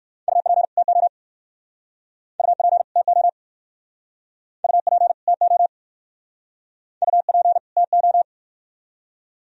Morse code, heard as a keyed tone, about 59 wpm, 700 Hz. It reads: UP TO UP TO UP TO UP TO